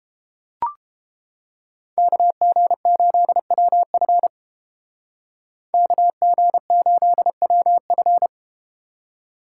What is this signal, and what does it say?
Morse code, heard as a keyed tone, about 31 wpm, 700 Hz. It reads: KG8WF KG8WF